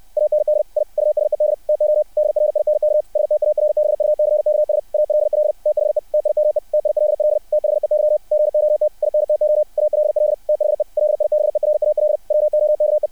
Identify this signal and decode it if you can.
Morse code, heard as a keyed tone, about 60 wpm, 600 Hz. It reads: AND I USED TO DRINK STARBUCKS ALL THE TIME TILL THEY GOT INTO ALL THE LIBERAL WOKE
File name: archived / CRSnet